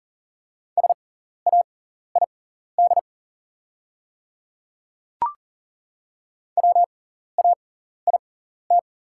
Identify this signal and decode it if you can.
Morse code, heard as a keyed tone, about 40 wpm, spaced as for 14 wpm, 700 Hz. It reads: SAID WAIT
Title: 6x Character Spacing - Top 500 Words - 40wpm